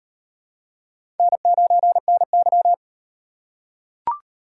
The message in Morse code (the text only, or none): N9NY